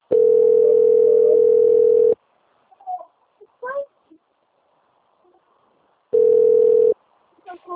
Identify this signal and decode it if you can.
Ringback tone (North America), 440 + 480 Hz, with speech.
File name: out- - - - - .926